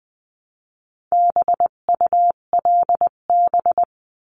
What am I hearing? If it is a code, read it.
Morse code, heard as a keyed tone, about 20 wpm, 700 Hz. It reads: BULB
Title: 258 Lesson 42c - Words 16-20wpm